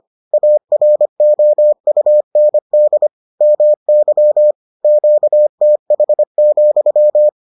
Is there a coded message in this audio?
Yes: Morse code from 0.3 to 7.3 s, heard as a keyed tone, about 25 wpm, 600 Hz. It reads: AROUND MY QTH,